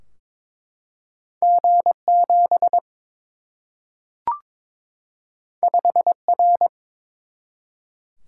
Morse code, heard as a keyed tone, about 22 wpm, 700 Hz. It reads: G7 5R
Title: Single-Letter Single-Number 22wpm